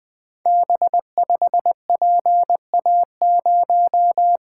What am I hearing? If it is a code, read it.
Morse code, heard as a keyed tone, about 20 wpm, 700 Hz. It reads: B5PA0
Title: Code Groups - 5 Letter:Numbers - 20wpm